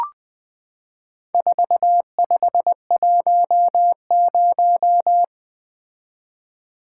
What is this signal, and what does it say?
Morse code, heard as a keyed tone, about 20 wpm, 700 Hz. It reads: 4510
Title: Four Letter-Number Combinations - 20wpm